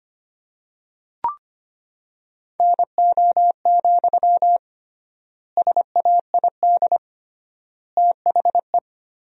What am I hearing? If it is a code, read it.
Morse code, heard as a keyed tone, about 25 wpm, 700 Hz. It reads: NO, SAID THE
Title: 3x Word-Spacing - Sentences from Top 300 Words - 25wpm